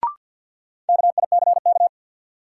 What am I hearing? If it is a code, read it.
Morse code, heard as a keyed tone, about 50 wpm, 700 Hz. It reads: KICK
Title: Sentences from Top 1300 Words - New Word Drill - 50wpm